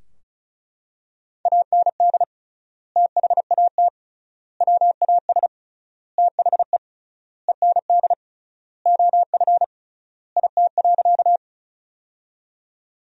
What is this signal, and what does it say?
Morse code, heard as a keyed tone, about 34 wpm, 700 Hz. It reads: AND THAT WAS THE END OF IT.
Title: 3x Word-Spacing - Sentences from Top 200 Words - 35wpm